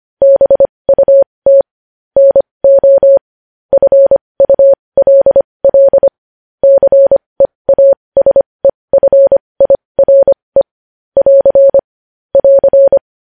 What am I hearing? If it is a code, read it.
Morse code, heard as a keyed tone, about 25 wpm, spaced as for 20 wpm, 550 Hz. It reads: BUT NO FULL CEASEFIRE <AR> <AR>
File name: podcast / News headlines-1x-25x20